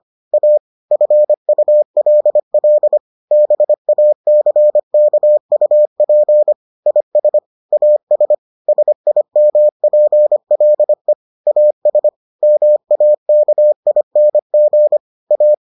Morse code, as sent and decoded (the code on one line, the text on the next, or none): .- / ..-. ..- .-.. .-.. / -... .- -.-. -.- ..- .--. / .. ... / .- ... / ... .. -- .--. .-.. / .- ... / -- .- -.- .. -. --. / .-
A FULL BACKUP IS AS SIMPLE AS MAKING A